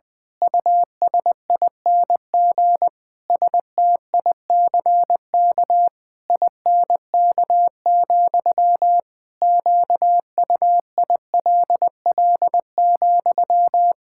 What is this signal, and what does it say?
Morse code, heard as a keyed tone, about 20 wpm, 700 Hz. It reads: USING STICK INK, QUILL,
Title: Pangram Challenge v2 - 20wpm